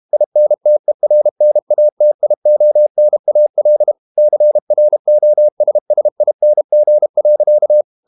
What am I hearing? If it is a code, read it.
Morse code, heard as a keyed tone, about 32 wpm, 600 Hz. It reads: INTERNATIONAL CROSSING.
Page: Never 31wpm | CW med Gnister